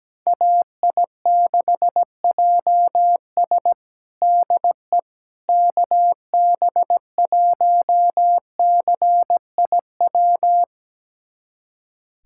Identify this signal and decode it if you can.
Morse code, heard as a keyed tone, about 17 wpm, 700 Hz. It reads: AI6JS DE KB1CIW
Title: USA Callsign DE Callsign 17wpm